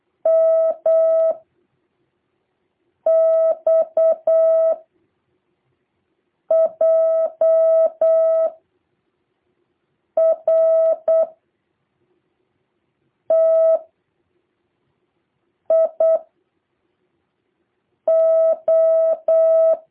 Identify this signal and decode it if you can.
Morse code, heard as a keyed tone, about 8 wpm, 650 Hz. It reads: M X J R T I O
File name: jh8cbh / 2023cw